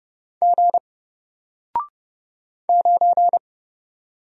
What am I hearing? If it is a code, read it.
Morse code, heard as a keyed tone, about 29 wpm, 700 Hz. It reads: G 9
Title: Single Character (Letter:Number:Punctuation:Prosign) 30wpm